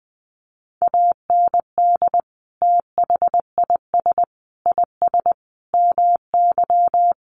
Morse code, heard as a keyed tone, about 20 wpm, 700 Hz. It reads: AND THIS IS MY
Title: Sentences from Top 1200 Words 20wpm